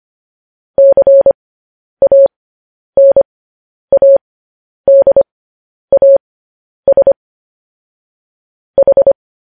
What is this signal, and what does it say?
Morse code, heard as a keyed tone, about 25 wpm, spaced as for 10 wpm, 550 Hz. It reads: CANADAS H
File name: podcast / News headlines-1x-25x10